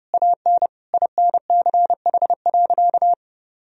Morse code, heard as a keyed tone, about 30 wpm, 700 Hz. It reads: AN INCH.
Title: Sentences from Top 1000 Words - Review Part 3 - 30wpm